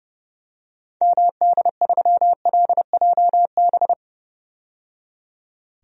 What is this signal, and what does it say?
Morse code, heard as a keyed tone, about 30 wpm, 700 Hz. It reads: MD3LJB
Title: DXCC-100 Call Signs - 6 Characters - 30wpm